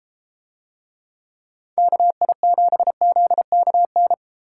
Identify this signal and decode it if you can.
Morse code, heard as a keyed tone, about 31 wpm, 700 Hz. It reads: KI7ZKN